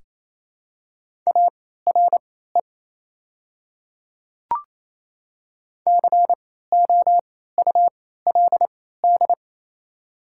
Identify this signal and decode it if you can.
Morse code, heard as a keyed tone, about 28 wpm, spaced as for 16 wpm, 700 Hz. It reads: ARE COULD
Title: 3x Character Spacing - Top 100 Words - 28wpm